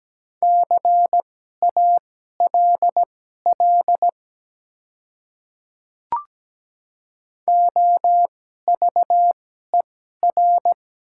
Morse code, heard as a keyed tone, about 17 wpm, spaced as for 12 wpm, 700 Hz. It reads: CALL OVER
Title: 2x Character Spacing - Top 100 Words - 17wpm